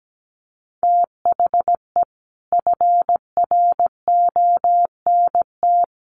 Morse code, heard as a keyed tone, about 17 wpm, 700 Hz. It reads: THE FRONT
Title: Sets of 5 Words Encore 17wpm